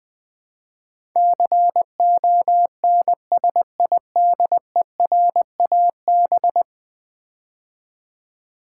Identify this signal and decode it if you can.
Morse code, heard as a keyed tone, about 20 wpm, 700 Hz. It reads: CONSIDERAB
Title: Build A Word - 5 Letters At A Time - 12-Letter Words - 20wpm